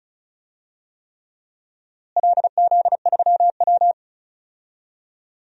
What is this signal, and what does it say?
Morse code, heard as a keyed tone, about 36 wpm, 700 Hz. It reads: LZ3W
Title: DXCC-100 Call Signs - 4 Characters - Repeated 3x - 35wpm